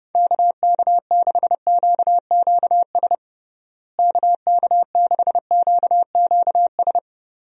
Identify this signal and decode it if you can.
Morse code, heard as a keyed tone, about 32 wpm, 700 Hz. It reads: KK6QQS KK6QQS
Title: USA Call Signs - 6 Characters - Repeated 2x - 30wpm